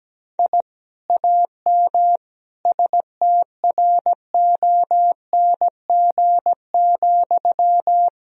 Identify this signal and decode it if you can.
Morse code, heard as a keyed tone, about 17 wpm, 700 Hz. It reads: I AM STRONG,
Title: Sentences from Top 1000 Words - Review Part 3 - 17wpm